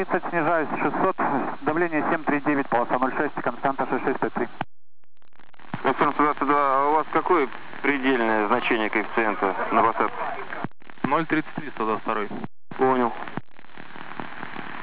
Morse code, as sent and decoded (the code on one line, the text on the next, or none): none